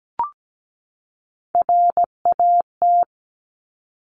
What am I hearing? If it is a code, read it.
Morse code, heard as a keyed tone, about 17 wpm, 700 Hz. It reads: RAT
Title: Top Three Letter Combinations in Words 17wpm